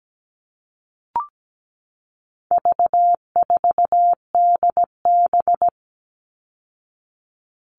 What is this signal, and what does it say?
Morse code, heard as a keyed tone, about 17 wpm, 700 Hz. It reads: V4DB
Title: All Cut Numbers - 4-Digit 17wpm